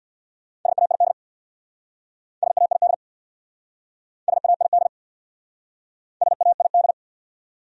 Morse code, heard as a keyed tone, about 60 wpm, 690 Hz. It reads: SAID SAID SAID SAID
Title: Speed-Racing Top 100 Words 50wpm